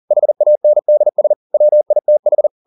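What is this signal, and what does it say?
Morse code, heard as a keyed tone, about 40 wpm, 600 Hz. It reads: HANDS WITH
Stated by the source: Morse code at 40 wpm, practice text, CW med Gnister.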